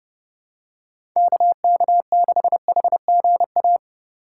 Morse code, heard as a keyed tone, about 32 wpm, 700 Hz. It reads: KK6HGA
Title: USA Call Signs - 3x Character Spacing - 30wpm